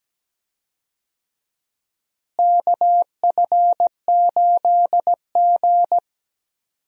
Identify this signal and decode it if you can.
Morse code, heard as a keyed tone, about 17 wpm, 700 Hz. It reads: KF8G
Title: USA Call Signs - 4 Characters - Repeated 2x - 17wpm